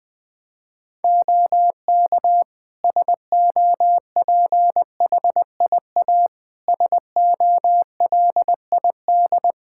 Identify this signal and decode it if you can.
Morse code, heard as a keyed tone, about 20 wpm, 700 Hz. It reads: OK SOPHIA SOLID
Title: Practice QSO - Exchange 3 - More Info - 20wpm